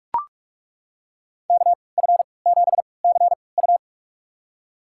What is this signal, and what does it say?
Morse code, heard as a keyed tone, about 41 wpm, spaced as for 25 wpm, 700 Hz. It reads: KF7CU
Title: USA Call Signs - 5 Characters - 3x Character Spacing - 45wpm